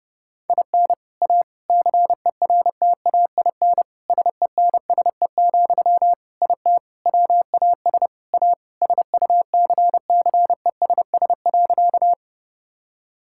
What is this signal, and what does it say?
Morse code, heard as a keyed tone, about 28 wpm, 700 Hz. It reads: IN A CERTAIN SENSE, IT WAS A SUCCESS.
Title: Sentences from Top 1000 Words - Review Part 6 - 30wpm